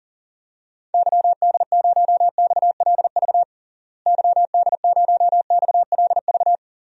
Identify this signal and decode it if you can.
Morse code, heard as a keyed tone, about 40 wpm, 700 Hz. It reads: YD0XLV YD0XLV
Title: DXCC-100 Call Signs - 6 Characters - Repeated 3x - 40wpm